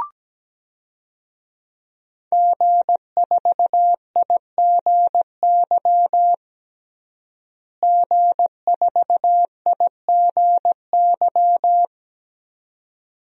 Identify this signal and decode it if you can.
Morse code, heard as a keyed tone, about 17 wpm, 700 Hz. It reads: G4IGY G4IGY
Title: DXCC-100 Call Signs - 5 Characters - Repeated 2x - 17wpm